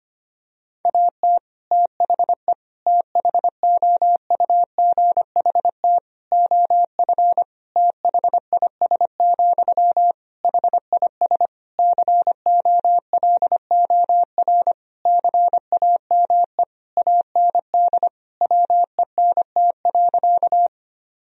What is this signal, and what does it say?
Morse code, heard as a keyed tone, about 25 wpm, 700 Hz. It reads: AT THE THOUGHT OF THIS, HIS COLOR CAME AND WENT.
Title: Sentences from Top 1000 Words - Review Part 2 - 25wpm